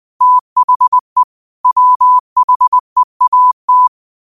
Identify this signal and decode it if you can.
Morse code, heard as a keyed tone, about 20 wpm, 1000 Hz. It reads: THE WHEAT
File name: qotdfiles / 20wpm